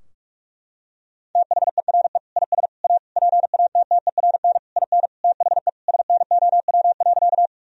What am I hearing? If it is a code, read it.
Morse code, heard as a keyed tone, about 46 wpm, 700 Hz. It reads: THERE IS A PATTERN IN THE SNOW.
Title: Sentences from Top 1000 Words - Review Part 2 - 45wpm